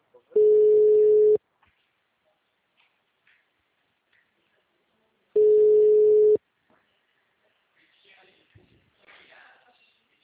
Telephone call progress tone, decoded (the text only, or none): ringback tone